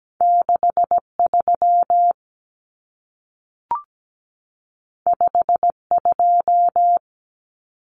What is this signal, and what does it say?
Morse code, heard as a keyed tone, about 17 wpm, 700 Hz. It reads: 63 52
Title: Numbers - 2-Digit 17wpm